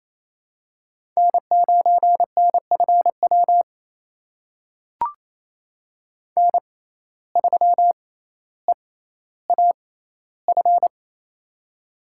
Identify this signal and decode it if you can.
Morse code, heard as a keyed tone, about 28 wpm, spaced as for 14 wpm, 700 Hz. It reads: N9NFW N3EAF
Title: USA Call Signs - 5 Characters - 6x Character Spacing - 28wpm